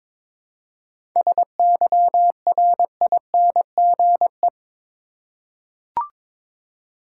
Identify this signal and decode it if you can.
Morse code, heard as a keyed tone, about 22 wpm, 690 Hz. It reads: SYRINGE
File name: Difficult Words with Hints - 22wpm